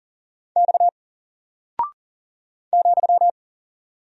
Morse code, heard as a keyed tone, about 40 wpm, 700 Hz. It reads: X ,
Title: Single Character (Letter:Number:Punctuation:Prosign) 40wpm